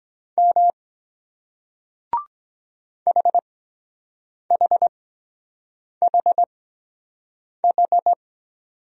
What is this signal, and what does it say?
Morse code, heard as a keyed tone, about 23 wpm, spaced as for 7 wpm, 700 Hz. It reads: M HHHH